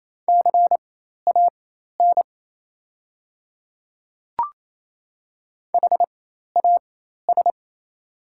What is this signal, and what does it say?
Morse code, heard as a keyed tone, about 28 wpm, spaced as for 13 wpm, 700 Hz. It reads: CAN HAS
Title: 4x Character Spacing - Top 100 Words - 28wpm